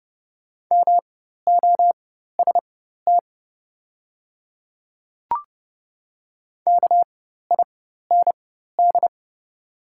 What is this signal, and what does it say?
Morse code, heard as a keyed tone, about 30 wpm, spaced as for 14 wpm, 690 Hz. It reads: MOST KIND
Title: 4x Character Spacing - Top 200 Words - 30wpm